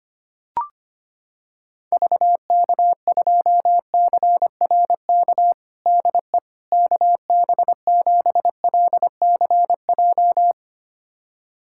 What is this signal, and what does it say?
Morse code, heard as a keyed tone, about 25 wpm, 700 Hz. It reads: VK2CRK DE KB7LCJ
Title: USA Callsign DE Callsign 25wpm